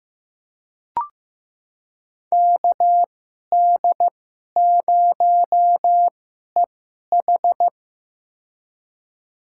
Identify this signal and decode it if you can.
Morse code, heard as a keyed tone, about 16 wpm, spaced as for 11 wpm, 700 Hz. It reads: KD0EH
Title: USA Call Signs - 5 Characters - 2x Character Spacing - 15wpm